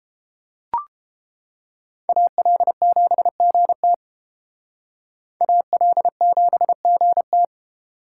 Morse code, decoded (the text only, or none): AL7GT AL7GT